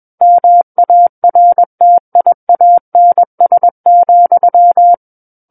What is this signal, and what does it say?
Morse code, heard as a keyed tone, about 21 wpm, 700 Hz. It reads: MARTIANS,
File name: Book One / 12-Chapter 12 - 21 WPM